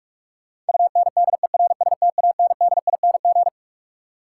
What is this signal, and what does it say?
Morse code, heard as a keyed tone, about 46 wpm, 700 Hz. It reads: UNDERSTANDING
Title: Sentences from Top 1400 Words - New Word Drill Alt - 45wpm